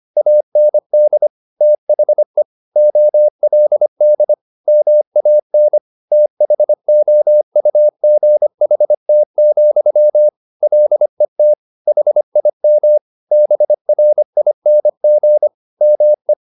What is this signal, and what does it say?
Morse code, heard as a keyed tone, about 25 wpm, 600 Hz. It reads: AND THE OLD MAN THOUGHT, LET HIM BRING ME